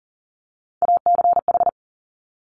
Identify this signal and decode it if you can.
Morse code, heard as a keyed tone, about 40 wpm, 700 Hz. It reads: ACH
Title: Top Three Letter Combinations in Words 40wpm